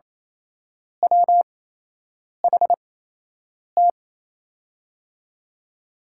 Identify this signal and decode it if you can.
Morse code, heard as a keyed tone, about 28 wpm, spaced as for 8 wpm, 700 Hz. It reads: WHT
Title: USA Call Signs - 8x Character Spacing - 28wpm